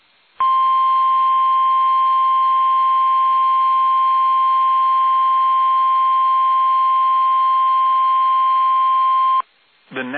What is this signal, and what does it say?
Emergency Alert System (SAME) tones (the 1050 Hz attention tone). It is heard from 0.4 to 9.4 s.